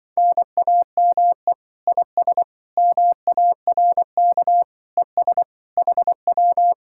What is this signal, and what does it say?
Morse code, heard as a keyed tone, about 24 wpm, 700 Hz. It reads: NAME IS MARK ES HW
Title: Speed-Racing Common QSO Phrases - 20wpm